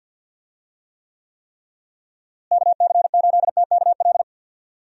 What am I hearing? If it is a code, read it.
Morse code, heard as a keyed tone, about 50 wpm, 700 Hz. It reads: KC8TKL
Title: USA Call Signs - 6 Characters - Repeated 3x - 50wpm